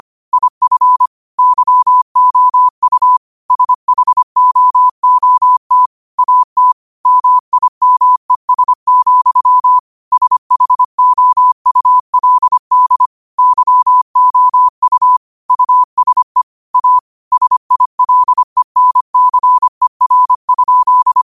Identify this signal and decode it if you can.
Morse code, heard as a keyed tone, about 25 wpm, 1000 Hz. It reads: IF YOU SHOOT AT MIMES, SHOULD YOU USE A SILENCER?